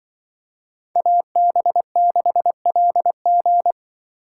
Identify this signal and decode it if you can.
Morse code, heard as a keyed tone, about 24 wpm, 700 Hz. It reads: AB6LG
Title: Speed Racing - USA 2x2 Call Signs 20wpm